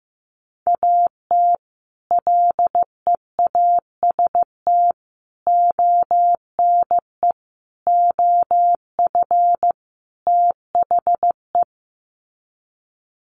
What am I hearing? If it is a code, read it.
Morse code, heard as a keyed tone, about 14 wpm, 700 Hz. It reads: AT LEAST ONE OF THE
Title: Sets of 5 Words Encore 15wpm